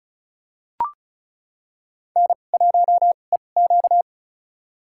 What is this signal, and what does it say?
Morse code, heard as a keyed tone, about 34 wpm, spaced as for 25 wpm, 700 Hz. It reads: N1EQ
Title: USA Call Signs - 4 Characters - 2x Character Spacing - 35wpm